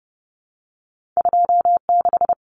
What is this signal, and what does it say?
Morse code, heard as a keyed tone, about 30 wpm, 700 Hz. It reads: 26
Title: Common Cut Numbers - 2-Digit 30wpm